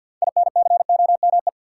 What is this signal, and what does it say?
Morse code, heard as a keyed tone, about 50 wpm, 700 Hz. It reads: INCOME
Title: Six Letter Words - 50wpm